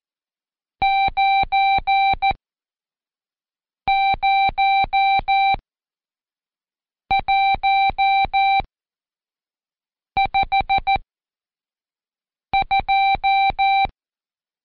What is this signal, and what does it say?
Morse code, heard as a keyed tone, about 14 wpm, spaced as for 5 wpm, 780 Hz. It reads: 90152
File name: Code Quick / AudioDownld2 / Seg91-Day19-Review Numb